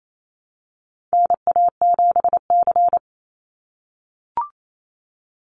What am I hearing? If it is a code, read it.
Morse code, heard as a keyed tone, about 28 wpm, 700 Hz. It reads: NA7C
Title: Call Signs - 28wpm